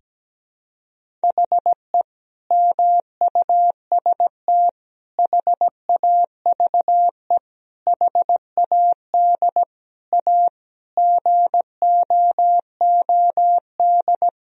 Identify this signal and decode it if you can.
Morse code, heard as a keyed tone, about 17 wpm, 700 Hz. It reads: HE MUST HAVE HAD A GOOD
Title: Sentences from Top 1000 Words - Review Part 1 - 17wpm